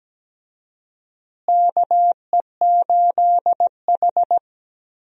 Morse code, heard as a keyed tone, about 17 wpm, 700 Hz. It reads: KE8H
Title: USA Call Signs - 4 Characters - Repeated 3x - 17wpm